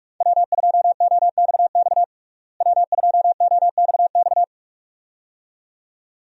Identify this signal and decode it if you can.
Morse code, heard as a keyed tone, about 43 wpm, 700 Hz. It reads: W2OXX W2OXX
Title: USA Call Signs - 5 Characters - Repeated 2x - 45wpm